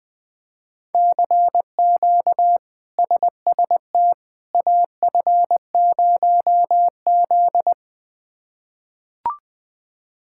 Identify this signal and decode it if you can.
Morse code, heard as a keyed tone, about 20 wpm, 700 Hz. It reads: CQ SST AF0Z
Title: K1USN SST QSO Practice - 20wpm